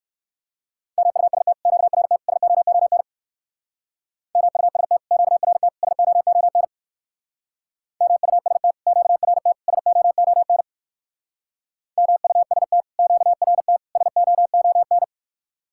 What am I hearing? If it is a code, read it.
Morse code, heard as a keyed tone, about 61 wpm, 700 Hz. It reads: MUST QRT SOON MUST QRT SOON MUST QRT SOON MUST QRT SOON